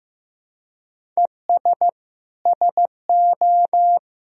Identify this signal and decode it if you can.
Morse code, heard as a keyed tone, about 15 wpm, 700 Hz. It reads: ES SO
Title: Sentences from Top 400 Words and Abbreviations 15wpm